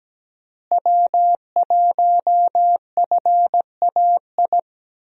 Morse code, heard as a keyed tone, about 17 wpm, 700 Hz. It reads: W1FAI
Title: USA Call Signs - 5 Characters - Repeated 2x - 17wpm